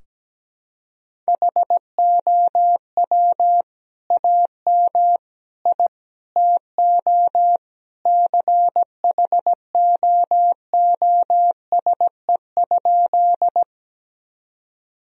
Morse code, heard as a keyed tone, about 17 wpm, 700 Hz. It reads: HOW AM I TO CHOOSE?
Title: Sentences from Top 1000 Words - Review Part 4 - 17wpm